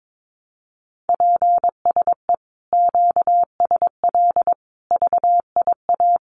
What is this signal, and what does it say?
Morse code, heard as a keyed tone, about 22 wpm, 700 Hz. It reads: PSE QSL VIA
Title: Common QSO Phrases Encore 22wpm